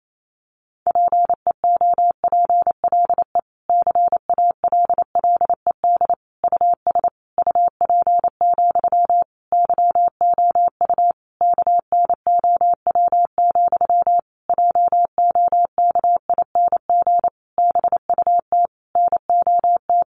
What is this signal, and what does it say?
Morse code, heard as a keyed tone, about 28 wpm, 700 Hz. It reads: PEOPLE CALLED US UP, YOU KNOW, JOKING BUT NOT
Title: A Sound Of Thunder 28wpm